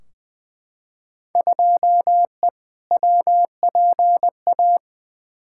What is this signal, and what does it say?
Morse code, heard as a keyed tone, about 20 wpm, 700 Hz. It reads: 2E WPA
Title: Field Day Exchanges Encore - 20wpm